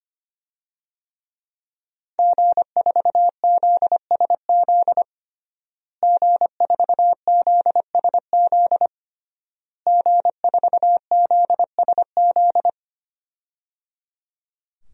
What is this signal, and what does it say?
Morse code, heard as a keyed tone, about 25 wpm, 700 Hz. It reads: G4ZSZ G4ZSZ G4ZSZ
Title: DXCC-100 Call Signs - 5 Characters - Repeated 3x - 25wpm